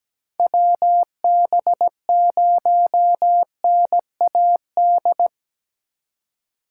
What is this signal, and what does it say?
Morse code, heard as a keyed tone, about 17 wpm, 700 Hz. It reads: WB0NAD
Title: USA Call Signs - 6 Characters - 4x Character Spacing - 17wpm